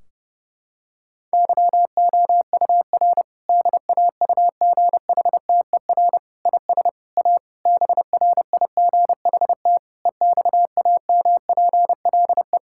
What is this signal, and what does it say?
Morse code, heard as a keyed tone, about 32 wpm, 700 Hz. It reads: YOUR DAUGHTER IS A BRIGHT EXAMPLE
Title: Sentences from Top 1600 Words 30wpm